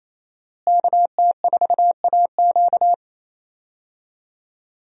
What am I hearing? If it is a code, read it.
Morse code, heard as a keyed tone, about 28 wpm, 700 Hz. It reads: KT4AQ
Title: USA Call Signs - 5 Characters - Repeated 3x - 28wpm